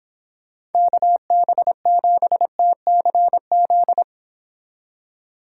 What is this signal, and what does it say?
Morse code, heard as a keyed tone, about 30 wpm, 700 Hz. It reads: KB7TCZ